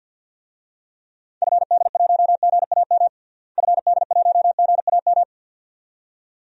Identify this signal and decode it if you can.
Morse code, heard as a keyed tone, about 52 wpm, 700 Hz. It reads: FD1GAM FD1GAM
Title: DXCC-100 Call Signs - 6 Characters - Repeated 2x - 50wpm